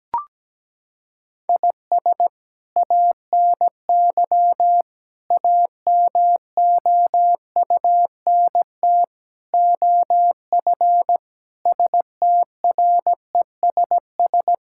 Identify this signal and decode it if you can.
Morse code, heard as a keyed tone, about 17 wpm, 700 Hz. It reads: IS ANY AMOUNT OF STRESS
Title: Sentences from Top 1000 Words - Review Part 6 - 17wpm